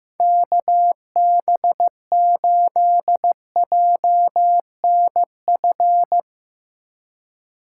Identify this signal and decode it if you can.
Morse code, heard as a keyed tone, about 15 wpm, 700 Hz. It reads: KB8JNF
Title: USA Call Signs - 6 Characters - 4x Character Spacing - 15wpm